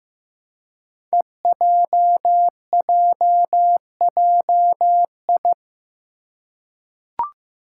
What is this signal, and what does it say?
Morse code, heard as a keyed tone, about 15 wpm, 700 Hz. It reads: EJJJI
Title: Five Challenging Letter-Number Combinations 15wpm